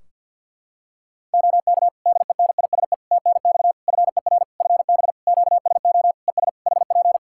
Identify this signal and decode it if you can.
Morse code, heard as a keyed tone, about 52 wpm, 700 Hz. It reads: OK DENISE TNX FER FB QSO ES HP
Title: K1USN SST QSO Practice 50wpm